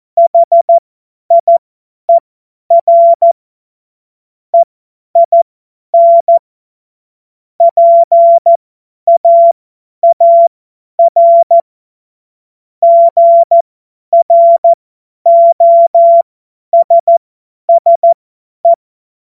Morse code, heard as a keyed tone, about 14 wpm, spaced as for 10 wpm, 680 Hz. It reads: HIER EIN PAAR GROSSE
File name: Morsekurs / Witze / Witz 17.Oktober 2023 50 F70